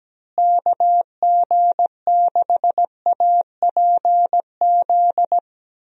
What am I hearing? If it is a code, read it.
Morse code, heard as a keyed tone, about 17 wpm, 700 Hz. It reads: KG6APZ
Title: USA Call Signs - 6 Characters - Repeated 3x - 17wpm